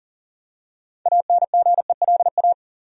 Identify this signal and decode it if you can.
Morse code, heard as a keyed tone, about 40 wpm, 700 Hz. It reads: ANGELA
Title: USA Common Names - 40wpm